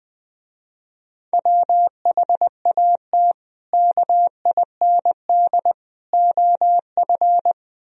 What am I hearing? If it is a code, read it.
Morse code, heard as a keyed tone, about 20 wpm, 700 Hz. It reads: WHAT KIND OF
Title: Speed-Racing - Sets of 3 Words 17wpm